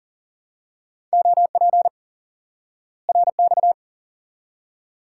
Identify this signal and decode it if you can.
Morse code, heard as a keyed tone, about 40 wpm, 700 Hz. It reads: OP RX
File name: IQE Course 3 - Lesson 89 - YAGI - Two Elements - 40wpm